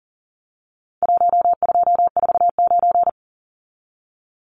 Morse code, heard as a keyed tone, about 40 wpm, 700 Hz. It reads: 1249
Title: Numbers - 4-Digit 40wpm